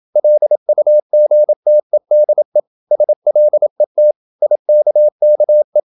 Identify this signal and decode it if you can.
Morse code, heard as a keyed tone, about 28 wpm, 600 Hz. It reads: LUGTEDE SLET IKKE
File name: Ferie QRQ 27wpm